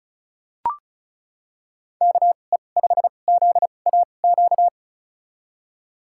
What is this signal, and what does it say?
Morse code, heard as a keyed tone, about 36 wpm, spaced as for 25 wpm, 700 Hz. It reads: KE5ZAQ
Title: USA Call Signs - 6 Characters - 2x Character Spacing - 35wpm